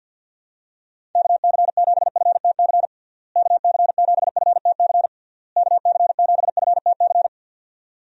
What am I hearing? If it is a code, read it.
Morse code, heard as a keyed tone, about 51 wpm, 700 Hz. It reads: KC7FTC KC7FTC KC7FTC
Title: USA Call Signs - 6 Characters - Repeated 3x - 50wpm